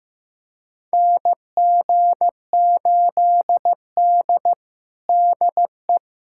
Morse code, heard as a keyed tone, about 15 wpm, 700 Hz. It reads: NG8D DE
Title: USA Callsign DE Callsign - 4 Characters - 15wpm